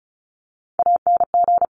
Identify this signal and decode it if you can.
Morse code, heard as a keyed tone, about 36 wpm, 700 Hz. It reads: ANG
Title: Top Three Letter Combinations in Words 35wpm